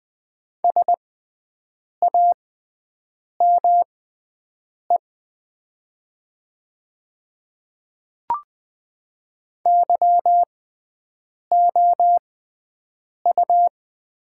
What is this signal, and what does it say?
Morse code, heard as a keyed tone, about 20 wpm, spaced as for 7 wpm, 700 Hz. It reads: SAME YOU
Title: 6x Character Spacing - Top 200 Words - 20wpm